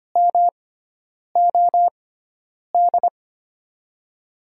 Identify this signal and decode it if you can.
Morse code, heard as a keyed tone, about 25 wpm, spaced as for 9 wpm, 700 Hz. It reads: MOD
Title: USA Call Signs - 6x Character Spacing - 25wpm